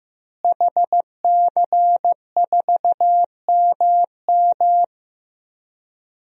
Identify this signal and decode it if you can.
Morse code, heard as a keyed tone, about 15 wpm, 700 Hz. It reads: HC4MM
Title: DXCC-100 Call Signs - 5 Characters - Repeated 2x - 15wpm